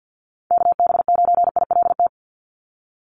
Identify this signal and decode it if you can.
Morse code, heard as a keyed tone, about 48 wpm, 700 Hz. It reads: KB9ILT
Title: NA SOTA Activator Callsigns 50wpm